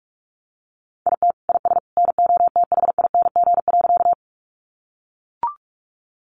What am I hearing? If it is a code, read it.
Morse code, heard as a keyed tone, about 44 wpm, 700 Hz. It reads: IT IS NOTHING.
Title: Sentences from Top 500 Words 45wpm